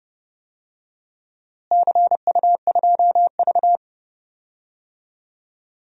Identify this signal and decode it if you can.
Morse code, heard as a keyed tone, about 30 wpm, 700 Hz. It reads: CU2V
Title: DXCC-100 Call Signs - 4 Characters - Repeated 2x - 30wpm